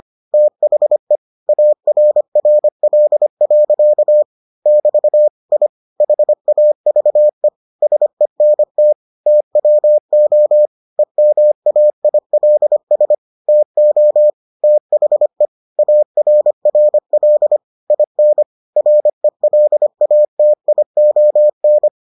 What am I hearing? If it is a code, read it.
Morse code, heard as a keyed tone, about 25 wpm, 600 Hz. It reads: THE ARRL. <BT> I HAVE SENT TWO EMAILS TO THE ARRL IN RELATION